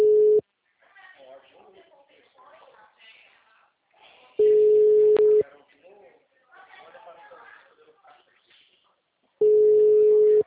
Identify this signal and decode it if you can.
Ringback tone (Europe, 425 Hz), with speech.
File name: out- -2005- - - .32